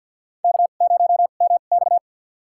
Morse code, heard as a keyed tone, about 52 wpm, spaced as for 37 wpm, 700 Hz. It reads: K0MX
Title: USA Call Signs - 4 Characters - 2x Character Spacing - 50wpm